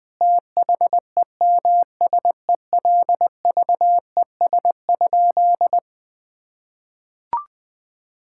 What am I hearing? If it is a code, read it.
Morse code, heard as a keyed tone, about 21 wpm, 700 Hz. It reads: THEMSELVES?
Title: Sentences from Top 1600 Words 20wpm